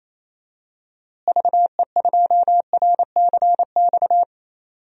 Morse code, heard as a keyed tone, about 28 wpm, 700 Hz. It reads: VE2RCX
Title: DXCC-100 Call Signs - 6 Characters - Repeated 3x - 28wpm